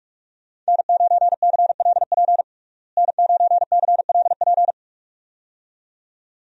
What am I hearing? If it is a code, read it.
Morse code, heard as a keyed tone, about 46 wpm, 700 Hz. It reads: N9CLP N9CLP
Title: USA Call Signs - 5 Characters - Repeated 2x - 45wpm